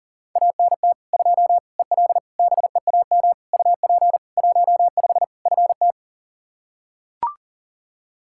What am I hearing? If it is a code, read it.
Morse code, heard as a keyed tone, about 40 wpm, 700 Hz. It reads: ANT 2 EL BEAM UP 15 FT
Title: Practice QSO - Exchange 2 - Info - 40wpm